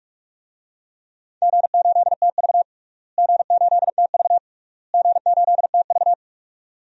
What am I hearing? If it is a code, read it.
Morse code, heard as a keyed tone, about 46 wpm, 700 Hz. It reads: G8TV G8TV G8TV
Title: DXCC-100 Call Signs - 4 Characters - Repeated 3x - 45wpm